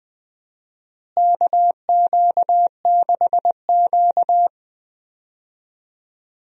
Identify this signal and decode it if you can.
Morse code, heard as a keyed tone, about 20 wpm, 700 Hz. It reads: KQ6Q
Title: USA Call Signs - 4 Characters - Repeated 3x - 20wpm